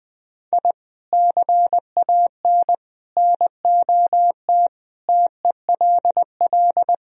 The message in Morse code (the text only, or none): I CAN NOT TELL